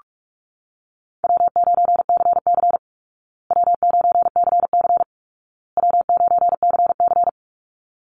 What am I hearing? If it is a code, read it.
Morse code, heard as a keyed tone, about 44 wpm, 700 Hz. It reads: W9CC W9CC W9CC
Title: Call Signs Repeated 2x - 45wpm